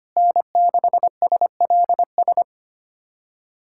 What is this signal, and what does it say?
Morse code, heard as a keyed tone, about 25 wpm, 700 Hz. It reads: N6SLS
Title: USA Callsign DE Callsign - 5 Characters - 25wpm